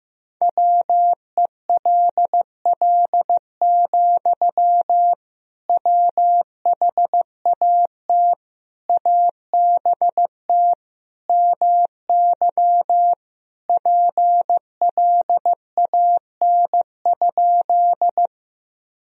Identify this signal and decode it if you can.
Morse code, heard as a keyed tone, about 15 wpm, 700 Hz. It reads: WELL, WHAT ABT MY PLAN?
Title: Sentences from Top 500 Words and Abbreviations 15wpm